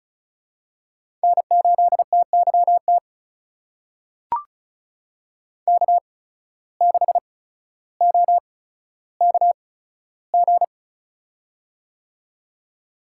Morse code, heard as a keyed tone, about 36 wpm, spaced as for 15 wpm, 700 Hz. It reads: N8TYT K6OKG
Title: USA Call Signs - 8x Character Spacing - 35wpm